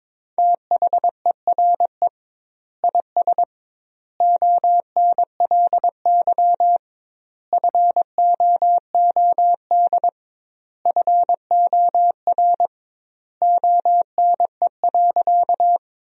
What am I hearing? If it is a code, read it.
Morse code, heard as a keyed tone, about 22 wpm, 700 Hz. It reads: THERE IS ONLY FOOD FOR ONE.
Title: 2x Word-Spacing - Sentences from Top 300 Words - 22wpm